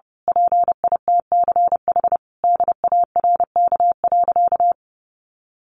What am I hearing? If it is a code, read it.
Morse code, heard as a keyed tone, about 30 wpm, 700 Hz. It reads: PITCH DARK.
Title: Sentences from Top 1300 Words 30wpm